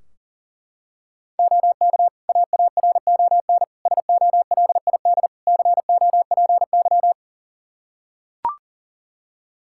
Morse code, heard as a keyed tone, about 37 wpm, 700 Hz. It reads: OK AARON SOLID COPY
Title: K1USN SST QSO Practice 40wpm